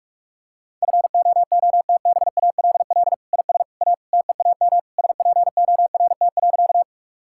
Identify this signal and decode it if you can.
Morse code, heard as a keyed tone, about 41 wpm, 700 Hz. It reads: FOOTBALL IS A TEAM SPORT.
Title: Sentences from Top 1600 Words 45wpm